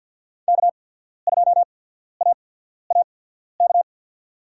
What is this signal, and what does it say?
Morse code, heard as a keyed tone, about 53 wpm, spaced as for 14 wpm, 700 Hz. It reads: K2AAK
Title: USA Call Signs - 5 Characters - 8x Character Spacing - 50wpm